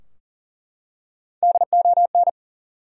Morse code, heard as a keyed tone, about 40 wpm, 700 Hz. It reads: DON
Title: USA Common Names - Max 5 Letters - 40wpm